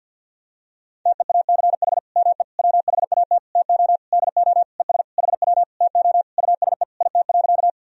Morse code, heard as a keyed tone, about 49 wpm, 700 Hz. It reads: TEACH ME WHAT TO DO ES HW TO USE IT.
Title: Sentences from Top 500 Words and Abbreviations 50wpm